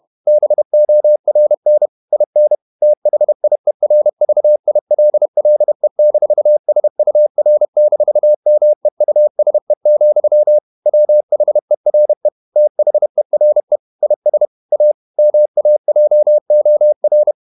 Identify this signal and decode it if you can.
Morse code, heard as a keyed tone, about 31 wpm, 600 Hz. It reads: BORN IN THIERVILLE-SUR-MEUSE, WHERE THERE IS A MAJOR
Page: Never 31wpm | CW med Gnister